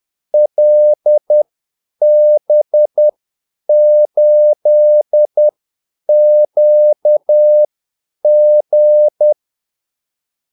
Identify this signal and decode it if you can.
Morse code, heard as a keyed tone, about 10 wpm, spaced as for 8 wpm, 600 Hz. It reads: LB8QG
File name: Callsigns-10-8wpm